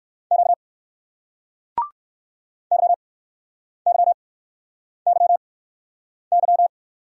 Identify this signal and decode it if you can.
Morse code, heard as a keyed tone, about 60 wpm, spaced as for 9 wpm, 700 Hz. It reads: Q YYYY